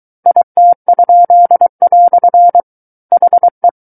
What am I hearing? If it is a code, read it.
Morse code, heard as a keyed tone, about 23 wpm, 700 Hz. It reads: IT?" HE
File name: Book One / 14-Chapter 14 - 23 WPM